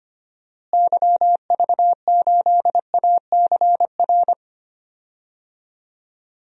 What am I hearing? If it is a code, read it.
Morse code, heard as a keyed tone, about 25 wpm, 700 Hz. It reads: YV8ACR
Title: DXCC-100 Call Signs - 6 Characters - Repeated 3x - 25wpm